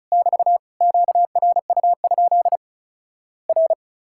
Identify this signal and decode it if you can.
Morse code, heard as a keyed tone, about 34 wpm, 700 Hz. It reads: <BT> QRU? R
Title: Worked All Europe DX Contest - QTC Exchange - 35wpm